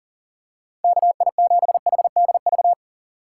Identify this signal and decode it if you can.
Morse code, heard as a keyed tone, about 40 wpm, 700 Hz. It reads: KI7HDV
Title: USA Call Signs - 6 Characters - 3x Character Spacing - 40wpm